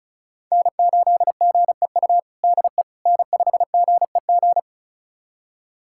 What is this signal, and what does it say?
Morse code, heard as a keyed tone, about 34 wpm, 700 Hz. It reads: N8GEU DE N5GEG
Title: USA Callsign DE Callsign - 5 Characters - 35wpm